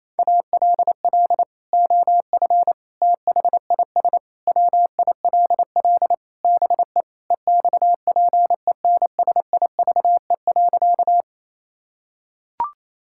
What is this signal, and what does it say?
Morse code, heard as a keyed tone, about 28 wpm, 700 Hz. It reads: ALL OF THIS WILL BE EXPENSIVE.
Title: Sentences from Top 1000 Words - Review Part 6 - 28wpm